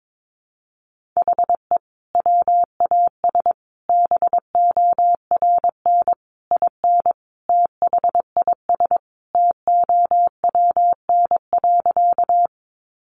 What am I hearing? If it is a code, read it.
Morse code, heard as a keyed tone, about 22 wpm, 700 Hz. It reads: HE WAS BORN IN THIS TOWN.
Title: Sentences from Top 1000 Words 22wpm